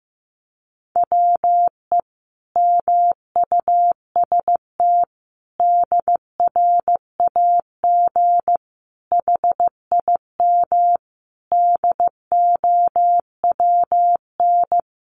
Morse code, heard as a keyed tone, about 15 wpm, 700 Hz. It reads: WE MUST DRAG HIM DOWN
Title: Sentences from Top 1500 Words 15wpm